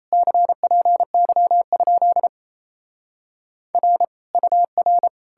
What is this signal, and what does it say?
Morse code, heard as a keyed tone, about 37 wpm, 700 Hz. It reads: CPY? R UR